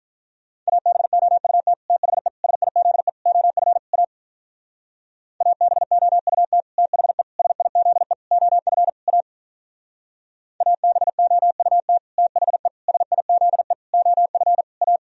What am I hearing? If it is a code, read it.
Morse code, heard as a keyed tone, about 46 wpm, 700 Hz. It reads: ABOUT THE SIZE OF A ABOUT THE SIZE OF A ABOUT THE SIZE OF A